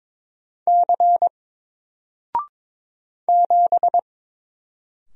Morse code, heard as a keyed tone, about 22 wpm, 700 Hz. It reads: C 7
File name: Single-Letter-Number-22wpm